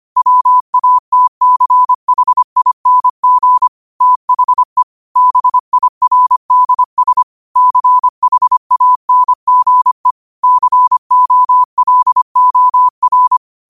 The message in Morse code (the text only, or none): WATCHING THE BIRDS CHANGE COLOR